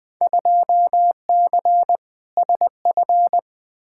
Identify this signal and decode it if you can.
Morse code, heard as a keyed tone, about 20 wpm, 700 Hz. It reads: 2C SF
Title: Field Day Exchanges Encore - 20wpm